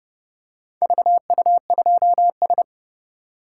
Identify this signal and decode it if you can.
Morse code, heard as a keyed tone, about 30 wpm, 700 Hz. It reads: VU2S